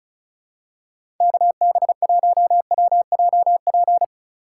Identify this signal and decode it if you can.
Morse code, heard as a keyed tone, about 36 wpm, 700 Hz. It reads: KB1WJP
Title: USA Call Signs - 6 Characters - 3x Character Spacing - 35wpm